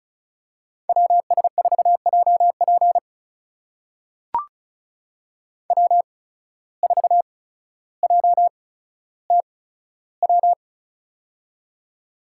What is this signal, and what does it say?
Morse code, heard as a keyed tone, about 36 wpm, spaced as for 15 wpm, 700 Hz. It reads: WS4JP W4JTW
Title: USA Call Signs - 5 Characters - 8x Character Spacing - 35wpm